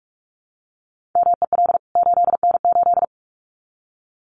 Morse code, heard as a keyed tone, about 44 wpm, 700 Hz. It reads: MEL 8N8
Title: CWT Exchange 45wpm